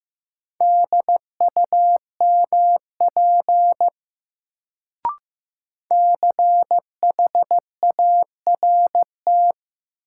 Morse code, heard as a keyed tone, about 15 wpm, 700 Hz. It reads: DUMP CHART
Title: Sentences from Top 1300 Words - New Word Drill - 15wpm